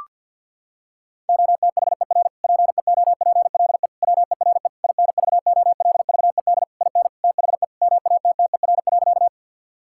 Morse code, heard as a keyed tone, about 50 wpm, 700 Hz. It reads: OTHER PEOPLE WERE INVOLVED IN THE MATTER.
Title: Sentences from Top 1000 Words - Review Part 5 - 50wpm